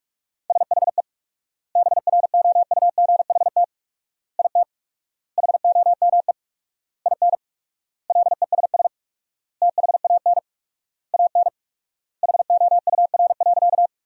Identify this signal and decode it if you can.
Morse code, heard as a keyed tone, about 46 wpm, 700 Hz. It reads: SHE BROUGHT IT HOME IN LESS THAN AN HOUR.
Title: 4x Word-Spacing - Sentences from Top 500 Words - 45wpm